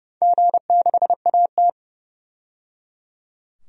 Morse code, heard as a keyed tone, about 29 wpm, 700 Hz. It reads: G6AT
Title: DXCC-100 Call Signs - 4 Characters - 30wpm